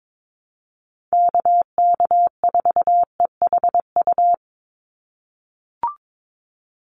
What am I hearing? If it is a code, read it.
Morse code, heard as a keyed tone, about 22 wpm, 700 Hz. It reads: KK4EHU
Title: USA Call Signs 22wpm